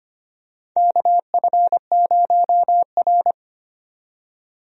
Morse code, heard as a keyed tone, about 25 wpm, 700 Hz. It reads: KF0R
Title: USA Call Signs - 4 Characters - 25wpm